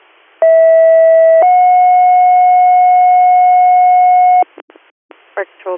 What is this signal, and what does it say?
Two-tone page, heard as a one-second tone, then a three-second tone, about 650 then 730 Hz, with speech. It is heard from 0.4 to 4.4 s.